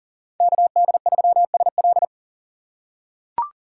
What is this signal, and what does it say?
Morse code, heard as a keyed tone, about 40 wpm, 700 Hz. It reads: KD3SL
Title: USA Call Signs - 5 Characters - 4x Character Spacing - 40wpm